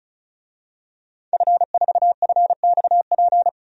Morse code, heard as a keyed tone, about 35 wpm, 700 Hz. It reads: F4FXP
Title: DXCC-100 Call Signs - 5 Characters - 35wpm